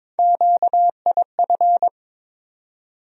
Morse code, heard as a keyed tone, about 22 wpm, 700 Hz. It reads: QIF
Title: USA Call Sign Postfixes - 3 Letters 22wpm